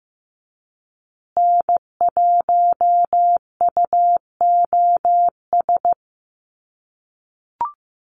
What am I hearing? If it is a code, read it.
Morse code, heard as a keyed tone, about 15 wpm, 700 Hz. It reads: N1UOS
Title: USA Call Signs Repeated 3x 15wpm